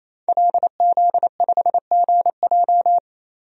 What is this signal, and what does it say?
Morse code, heard as a keyed tone, about 28 wpm, 700 Hz. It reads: LZ5GJ
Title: DXCC-100 Call Signs - 5 Characters - 28wpm